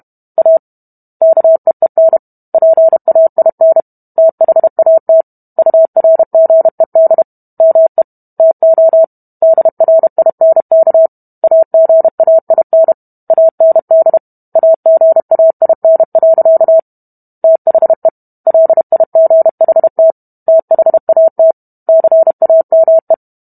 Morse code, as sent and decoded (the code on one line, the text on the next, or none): .- / -.- . -. / .--. .- .. -. / - .... .- - / ..- .-. --. -.. / -- . / - --- / -.. .-. .. -. -.- / .- --. .- .. -. / .- -. -.. / .- --. .- .. -. .-.-.- / - .... / .-.. .. --. .... - / - .... .- - / -.-. .- -- .
A KEEN PAIN THAT URGED ME TO DRINK AGAIN AND AGAIN. THE LIGHT THAT CAME